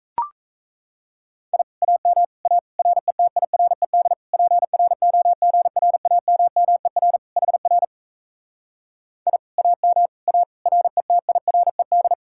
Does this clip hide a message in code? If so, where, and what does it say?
Yes: Morse code from 1.5 to 12.1 s, heard as a keyed tone, about 37 wpm, 700 Hz. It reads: I AM A RETIRED PROGRAMMER HR I AM A RETIRED